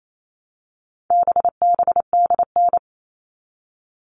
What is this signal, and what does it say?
Morse code, heard as a keyed tone, about 28 wpm, 700 Hz. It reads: BBDN
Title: All Cut Numbers - 4-Digit 28wpm